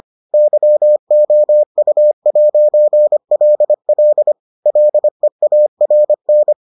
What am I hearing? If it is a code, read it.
Morse code, heard as a keyed tone, about 25 wpm, 600 Hz. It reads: YOU'LL LEARN